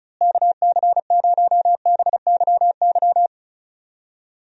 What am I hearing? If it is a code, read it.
Morse code, heard as a keyed tone, about 36 wpm, 700 Hz. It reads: KC0BYY
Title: USA Call Signs - 6 Characters - Repeated 2x - 35wpm